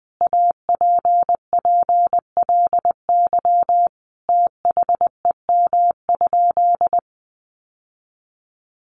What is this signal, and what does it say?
Morse code, heard as a keyed tone, about 20 wpm, 700 Hz. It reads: APPLY THEM?
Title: Sentences from Top 1400 Words 20wpm